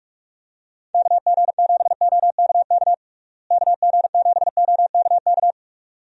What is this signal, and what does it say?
Morse code, heard as a keyed tone, about 46 wpm, 700 Hz. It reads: KG7OKK KG7OKK
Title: USA Call Signs - 6 Characters - Repeated 2x - 45wpm